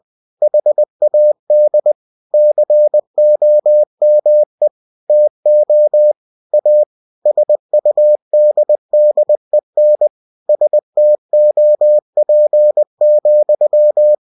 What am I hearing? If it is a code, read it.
Morse code, heard as a keyed tone, about 20 wpm, 600 Hz. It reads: HAD COME TO A SUDDEN STOP,
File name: cw-books / winnie-the-pooh / winnie-20wpm-page-0022